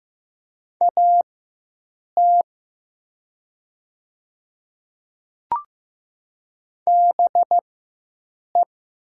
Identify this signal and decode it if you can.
Morse code, heard as a keyed tone, about 15 wpm, spaced as for 7 wpm, 700 Hz. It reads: AT BE
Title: 4x Character Spacing - Top 200 Words - 15wpm